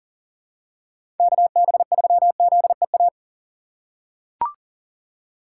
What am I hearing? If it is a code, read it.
Morse code, heard as a keyed tone, about 40 wpm, 700 Hz. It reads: KB3ZEA
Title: USA Call Signs - 2x Character Spacing - 40wpm